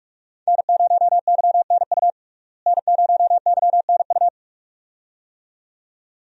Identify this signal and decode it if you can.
Morse code, heard as a keyed tone, about 46 wpm, 700 Hz. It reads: N0YNU N0YNU
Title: USA Call Signs - 5 Characters - Repeated 2x - 45wpm